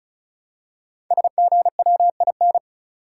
Morse code, heard as a keyed tone, about 36 wpm, 700 Hz. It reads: SGWIN
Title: Code Groups - 5 Letters - 35wpm